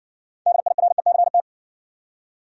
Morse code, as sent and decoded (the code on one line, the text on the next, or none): -.. .. .-. -.-. -
DIRECT